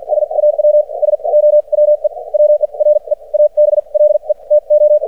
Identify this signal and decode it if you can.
Morse code, heard as a keyed tone, about 53 wpm, spaced as for 40 wpm, 600 Hz. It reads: SAW SO WE WERE ABLE TO
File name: archived / CRSnet